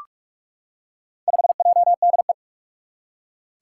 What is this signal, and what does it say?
Morse code, heard as a keyed tone, about 44 wpm, 700 Hz. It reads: 5JDE
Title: Four Challenging Letter-Number Combinations 45wpm